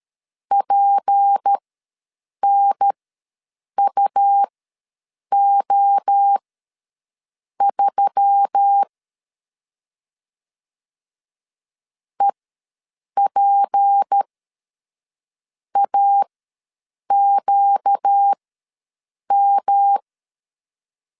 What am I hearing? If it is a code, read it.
Morse code, heard as a keyed tone, about 13 wpm, spaced as for 6 wpm, 800 Hz. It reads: PNUO3 EPAQM
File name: LEZIONI / 6 PKX0 / 6 P ws 12wpm 3117